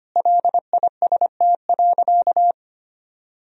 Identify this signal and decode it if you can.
Morse code, heard as a keyed tone, about 25 wpm, 700 Hz. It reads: LIST.
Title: Sentences from Top 1000 Words - Review Part 2 - 25wpm